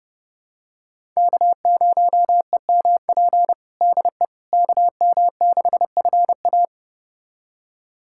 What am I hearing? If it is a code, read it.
Morse code, heard as a keyed tone, about 30 wpm, 700 Hz. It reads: K0EMP DE KM6FA
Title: USA Callsign DE Callsign - 5 Characters - 30wpm